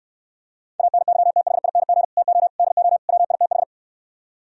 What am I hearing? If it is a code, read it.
Morse code, heard as a keyed tone, about 68 wpm, 700 Hz. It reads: ANYTHING TO DO WITH